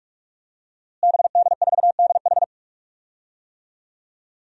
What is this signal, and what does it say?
Morse code, heard as a keyed tone, about 46 wpm, 700 Hz. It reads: BD4DH
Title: DXCC-100 Call Signs - 5 Characters - Repeated 2x - 45wpm